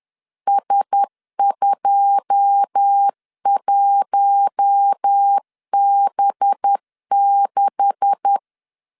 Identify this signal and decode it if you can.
Morse code, heard as a keyed tone, about 10 wpm, 800 Hz. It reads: S21B6
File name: CW / DET / 1 9 1500C S B 10wpm 4439